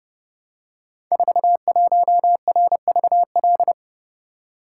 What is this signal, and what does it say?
Morse code, heard as a keyed tone, about 30 wpm, 700 Hz. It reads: 41RVL
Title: Code Groups - 5 Letter:Numbers - 30wpm